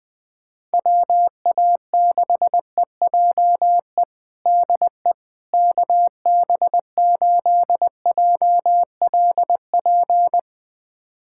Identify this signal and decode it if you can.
Morse code, heard as a keyed tone, about 20 wpm, 700 Hz. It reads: WA6EJE DE KB8JLP
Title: USA Callsign DE Callsign - 6 Characters - 20wpm